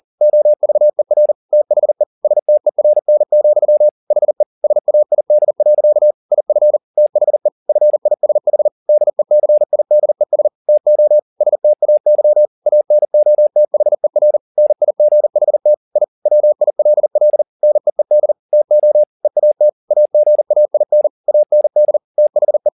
Morse code, heard as a keyed tone, about 40 wpm, 600 Hz. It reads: OVER THE STERN, HE SAID. IF THE FISH DECIDES TO STAY ANOTHER NIGHT I WILL NEED TO EAT AGAIN AND THE